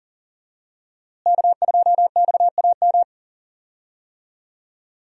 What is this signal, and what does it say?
Morse code, heard as a keyed tone, about 40 wpm, 690 Hz. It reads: K2XAM
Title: USA Call Signs - 5 Characters - Repeated 3x - 40wpm